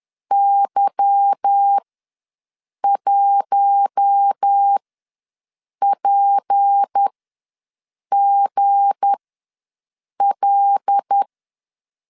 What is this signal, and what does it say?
Morse code, heard as a keyed tone, about 10 wpm, spaced as for 6 wpm, 800 Hz. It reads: Y1PGL